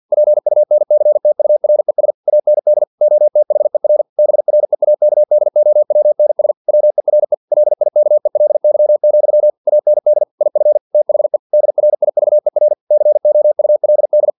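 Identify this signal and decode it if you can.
Morse code, heard as a keyed tone, about 50 wpm, 600 Hz. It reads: PUNCTURES AND OTHER BREAKDOWNS WERE LIKELY, AND IF THE DRIVER COULD
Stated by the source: Morse practice text from an English-language novel at 49 wpm.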